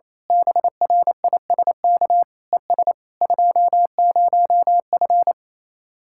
Morse code, heard as a keyed tone, about 26 wpm, 700 Hz. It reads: BRISK ES 20F
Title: Speed-Racing Common QSO Phrases - 28wpm